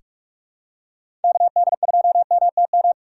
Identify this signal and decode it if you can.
Morse code, heard as a keyed tone, about 46 wpm, 700 Hz. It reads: KD2MTM
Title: USA Call Signs - 6 Characters - 2x Character Spacing - 45wpm